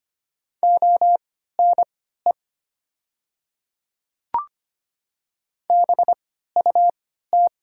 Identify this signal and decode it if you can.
Morse code, heard as a keyed tone, about 25 wpm, spaced as for 14 wpm, 700 Hz. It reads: ONE BUT
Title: 3x Character Spacing - Top 100 Words - 25wpm